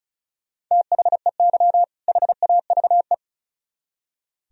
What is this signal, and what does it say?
Morse code, heard as a keyed tone, about 46 wpm, 700 Hz. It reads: THEY HAVE